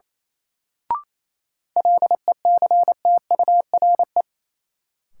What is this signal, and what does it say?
Morse code, heard as a keyed tone, about 28 wpm, 700 Hz. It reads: LECTURE
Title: Sentences from Top 1600 Words - New Word Drill Alt - 28wpm